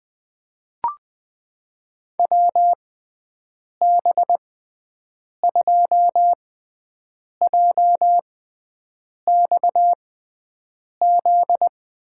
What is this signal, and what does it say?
Morse code, heard as a keyed tone, about 20 wpm, spaced as for 7 wpm, 700 Hz. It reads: WB2JXZ
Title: USA Call Signs - 6 Characters - 6x Character Spacing - 20wpm